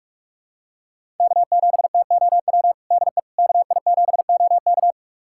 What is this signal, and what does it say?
Morse code, heard as a keyed tone, about 41 wpm, 700 Hz. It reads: K7TOW DE KI7OK
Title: USA Callsign DE Callsign - 5 Characters - 45wpm